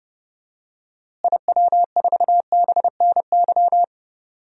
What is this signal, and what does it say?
Morse code, heard as a keyed tone, about 32 wpm, 700 Hz. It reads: IW4BNY
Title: DXCC-100 Call Signs - 6 Characters - Repeated 3x - 30wpm